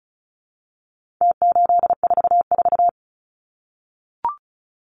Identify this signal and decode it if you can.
Morse code, heard as a keyed tone, about 34 wpm, 700 Hz. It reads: T844
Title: Numbers - Contest Style - 4-Digit 35wpm